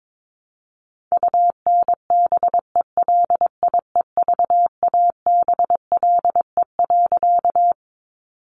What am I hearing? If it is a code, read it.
Morse code, heard as a keyed tone, about 22 wpm, 700 Hz. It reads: UNBELIEVABLE.
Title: A Sound Of Thunder 22wpm